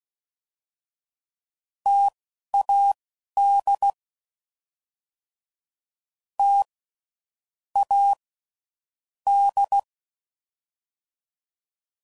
Morse code, heard as a keyed tone, about 16 wpm, spaced as for 12 wpm, 800 Hz. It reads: TAD T A D
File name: Mixed / Variable Sp / T AOEIs